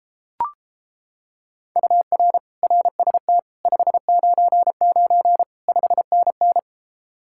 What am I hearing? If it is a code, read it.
Morse code, heard as a keyed tone, about 29 wpm, 700 Hz. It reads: UR RST 599 5NN